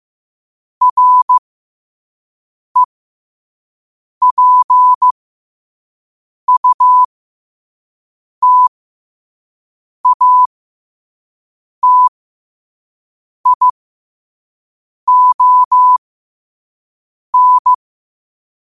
Morse code, heard as a keyed tone, about 15 wpm, spaced as for 5 wpm, 1000 Hz. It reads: REPUTATION
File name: qotdfiles / 5wpm